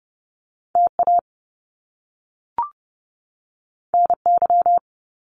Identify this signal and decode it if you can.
Morse code, heard as a keyed tone, about 30 wpm, 700 Hz. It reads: TA NY
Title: Top Two Letter Combinations in Words 30wpm